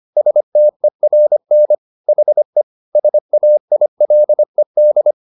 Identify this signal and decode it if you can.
Morse code, heard as a keyed tone, about 25 wpm, 600 Hz. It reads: STERN HE SAILED